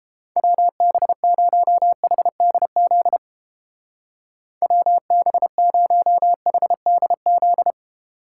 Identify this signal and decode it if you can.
Morse code, heard as a keyed tone, about 31 wpm, 700 Hz. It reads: WB0HDZ WB0HDZ